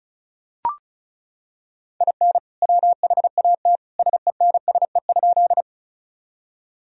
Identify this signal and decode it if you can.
Morse code, heard as a keyed tone, about 34 wpm, 700 Hz. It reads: IN WHAT SENSE?
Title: Sentences from Top 1000 Words - Review Part 4 - 35wpm